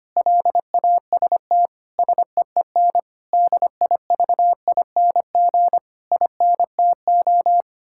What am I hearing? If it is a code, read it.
Morse code, heard as a keyed tone, about 25 wpm, 700 Hz. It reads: LAST SEEN DIVING INTO
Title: Pangram Challenge v2 - 25wpm